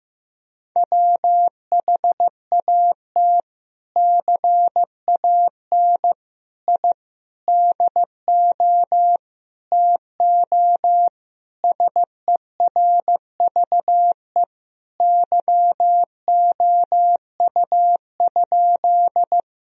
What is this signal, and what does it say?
Morse code, heard as a keyed tone, about 15 wpm, 700 Hz. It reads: WHAT CAN I DO TO SERVE YOU?
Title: Sentences from Top 1000 Words - Review Part 2 - 15wpm